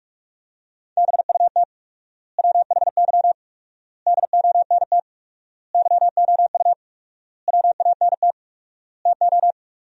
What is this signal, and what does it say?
Morse code, heard as a keyed tone, about 46 wpm, 700 Hz. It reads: BUT WHY DONT YOU WANT TO
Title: 4x Word-Spacing - Sentences from Top 300 Words - 45wpm